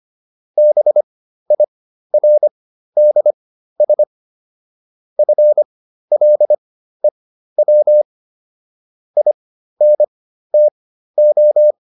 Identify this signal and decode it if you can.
Morse code, heard as a keyed tone, about 25 wpm, spaced as for 13 wpm, 600 Hz. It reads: BIRDS FLEW INTO